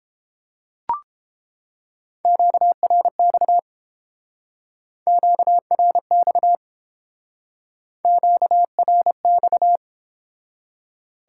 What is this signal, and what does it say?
Morse code, heard as a keyed tone, about 31 wpm, 700 Hz. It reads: QRX QRX QRX